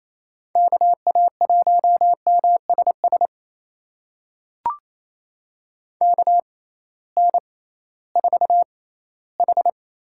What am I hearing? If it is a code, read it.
Morse code, heard as a keyed tone, about 28 wpm, spaced as for 16 wpm, 700 Hz. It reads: KA1MSS KN4H
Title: USA Call Signs - 6 Characters - 6x Character Spacing - 28wpm